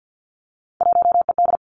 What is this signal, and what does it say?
Morse code, heard as a keyed tone, about 50 wpm, 700 Hz. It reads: 1ED
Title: Three Challenging Letter-Number Combinations 50wpm